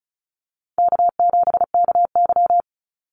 Morse code, heard as a keyed tone, about 34 wpm, 700 Hz. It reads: K7KY
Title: Call Signs - 35wpm